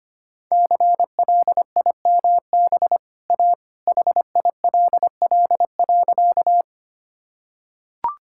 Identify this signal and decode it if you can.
Morse code, heard as a keyed tone, about 25 wpm, 690 Hz. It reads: CLIMB A HILL.
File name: Easy Sentences with Hints - 25wpm